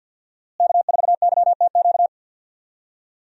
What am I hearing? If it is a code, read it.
Morse code, heard as a keyed tone, about 51 wpm, 700 Hz. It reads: K4YTQ
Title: USA Call Signs - 5 Characters - Repeated 3x - 50wpm